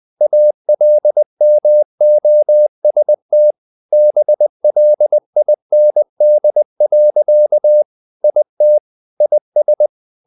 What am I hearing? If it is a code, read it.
Morse code, heard as a keyed tone, about 20 wpm, 600 Hz. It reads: ALMOST BLIND. IT IS
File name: the old man and the sea / the old man and the sea part 1 20wpm